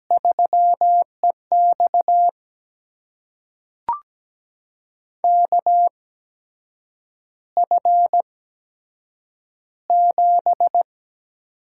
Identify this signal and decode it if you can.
Morse code, heard as a keyed tone, about 17 wpm, spaced as for 7 wpm, 700 Hz. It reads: 3EX KF7
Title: USA Call Signs - 8x Character Spacing - 17wpm